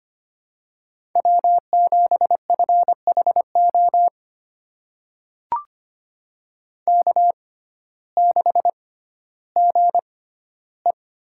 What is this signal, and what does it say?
Morse code, heard as a keyed tone, about 25 wpm, spaced as for 14 wpm, 700 Hz. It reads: W7FHO K6GE
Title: USA Call Signs - 6x Character Spacing - 25wpm